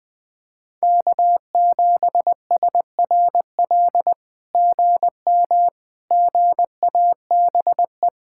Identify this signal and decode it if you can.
Morse code, heard as a keyed tone, about 20 wpm, 700 Hz. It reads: K7SRL GM GABE
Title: SOTA QSO Practice 20wpm